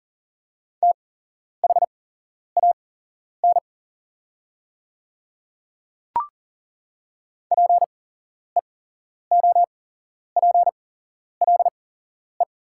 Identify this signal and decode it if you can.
Morse code, heard as a keyed tone, about 39 wpm, spaced as for 11 wpm, 700 Hz. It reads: THAN PEOPLE
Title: 8x Character Spacing - Top 100 Words - 40wpm